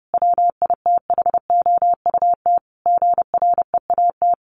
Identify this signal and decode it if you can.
Morse code, heard as a keyed tone, about 30 wpm, 700 Hz. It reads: WITHOUT GREAT
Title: Sentences from Top 1400 Words 30wpm